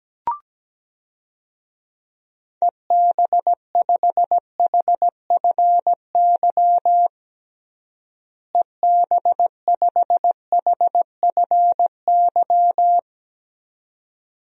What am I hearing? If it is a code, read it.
Morse code, heard as a keyed tone, about 17 wpm, 700 Hz. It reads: EB5HFY EB5HFY
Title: DXCC-100 Call Signs - 6 Characters - Repeated 2x - 17wpm